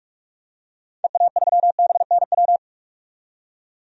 Morse code, heard as a keyed tone, about 44 wpm, 690 Hz. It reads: EA3BNW
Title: DXCC-100 Call Signs - 6 Characters - 45wpm